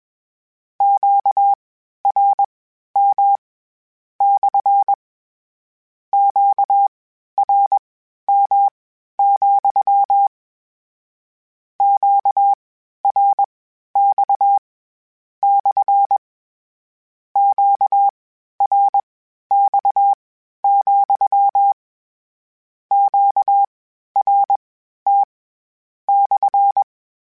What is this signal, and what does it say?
Morse code, heard as a keyed tone, about 21 wpm, spaced as for 12 wpm, 800 Hz. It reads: QRM / QRM, QRX / QRX, QRT /
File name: Q Calls / Q Calls-mp3s / Q Calls SPSL2 @20